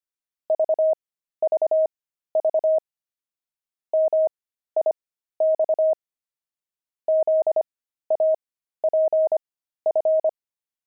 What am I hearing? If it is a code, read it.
Morse code, heard as a keyed tone, about 25 wpm, spaced as for 13 wpm, 630 Hz. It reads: VVV MIX ZAPF
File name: pangram-fs-13-cs-25-zaph